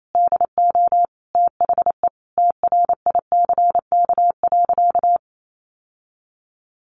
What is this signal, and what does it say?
Morse code, heard as a keyed tone, about 28 wpm, 700 Hz. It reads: DO THE TRICK.
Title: Sentences from Top 1500 Words 28wpm